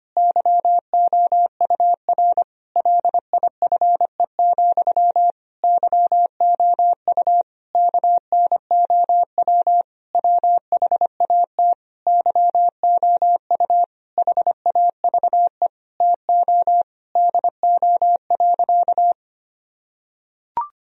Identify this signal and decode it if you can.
Morse code, heard as a keyed tone, about 25 wpm, 700 Hz. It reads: YOUR LIFE, YOU KNOW WHAT YOU HAVE TO DO.
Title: Sentences from Top 1000 Words - Review Part 3 - 25wpm